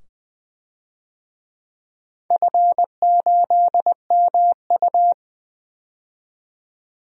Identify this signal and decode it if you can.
Morse code, heard as a keyed tone, about 20 wpm, 700 Hz. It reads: F8MU
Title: DXCC-100 Call Signs - 4 Characters - 20wpm